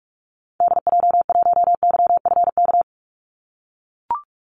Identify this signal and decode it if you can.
Morse code, heard as a keyed tone, about 44 wpm, 700 Hz. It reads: DJ1YFK
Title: Call Signs - 45wpm